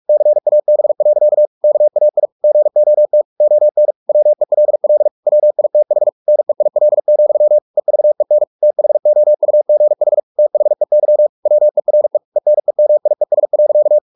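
Morse code, heard as a keyed tone, about 44 wpm, 600 Hz. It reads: CAB. KAI GOT ON WELL WITH NEIL, EVEN THOUGH THEY WERE ENEMIES.
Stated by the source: Morse practice at 45 wpm, the novel Never by Ken Follett in English, from CW med Gnister.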